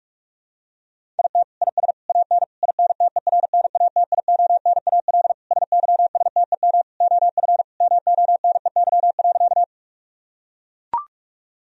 Morse code, heard as a keyed tone, about 42 wpm, 700 Hz. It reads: IT IS AN INTERNATIONAL SYSTEM OF MONEY.
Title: Sentences from Top 1000 Words - Review Part 4 - 45wpm